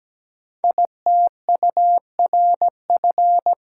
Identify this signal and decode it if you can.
Morse code, heard as a keyed tone, about 17 wpm, 700 Hz. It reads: ITURF
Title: Code Groups - 5 Letters - 17wpm